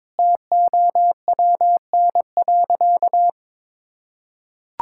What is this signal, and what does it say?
Morse code, heard as a keyed tone, about 22 wpm, 700 Hz. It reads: TOWN.
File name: practice / 22wpm / Sentences with 3-Word Phrases - 22wpm